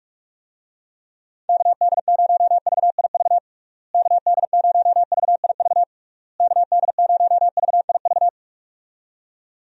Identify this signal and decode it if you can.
Morse code, heard as a keyed tone, about 42 wpm, 700 Hz. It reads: KD0VIV KD0VIV KD0VIV
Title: USA Call Signs - 6 Characters - Repeated 3x - 45wpm